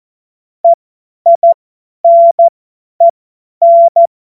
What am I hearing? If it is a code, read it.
Morse code, heard as a keyed tone, about 14 wpm, spaced as for 10 wpm, 680 Hz. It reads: EINEN
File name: Morsekurs / Witze / Witz 07. September 2021, 50 F70